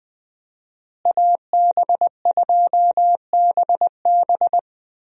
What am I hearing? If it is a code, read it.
Morse code, heard as a keyed tone, about 21 wpm, 700 Hz. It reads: AB2BB
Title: USA Call Signs - 5 Characters - Repeated 3x - 20wpm